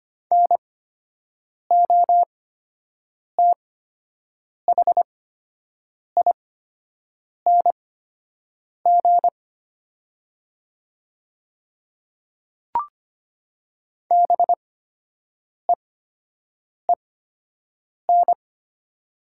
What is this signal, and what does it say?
Morse code, heard as a keyed tone, about 25 wpm, spaced as for 7 wpm, 700 Hz. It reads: NOTHING BEEN
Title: 8x Character Spacing - Top 500 Words - 25wpm